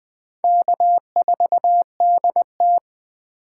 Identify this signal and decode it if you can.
Morse code, heard as a keyed tone, about 20 wpm, 700 Hz. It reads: K4DT